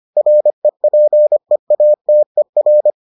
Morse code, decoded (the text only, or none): REPEATER